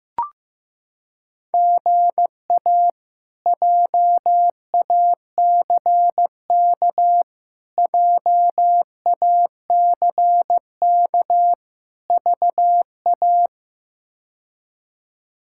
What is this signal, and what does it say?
Morse code, heard as a keyed tone, about 15 wpm, 700 Hz. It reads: GA JACK JACK VA
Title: K1USN SST QSO Practice - 15wpm